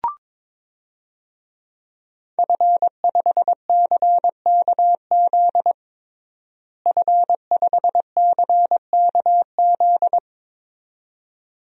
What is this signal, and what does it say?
Morse code, heard as a keyed tone, about 21 wpm, 700 Hz. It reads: F5CKZ F5CKZ
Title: DXCC-100 Call Signs - 5 Characters - Repeated 2x - 22wpm